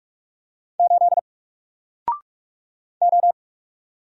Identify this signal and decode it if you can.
Morse code, heard as a keyed tone, about 45 wpm, 700 Hz. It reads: 8 O
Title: Single Letter-Number 45wpm